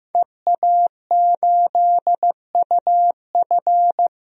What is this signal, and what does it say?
Morse code, heard as a keyed tone, about 15 wpm, 700 Hz. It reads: EA8UF
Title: DXCC-100 Call Signs - 5 Characters - 15wpm